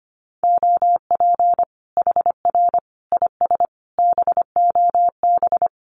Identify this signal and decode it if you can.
Morse code, heard as a keyed tone, about 25 wpm, 700 Hz. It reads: OP HR IS BOB
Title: Common QSO Phrases 25wpm